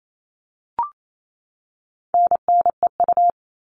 Morse code, heard as a keyed tone, about 28 wpm, 700 Hz. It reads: NNEU
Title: All Cut Numbers - 4-Digit 28wpm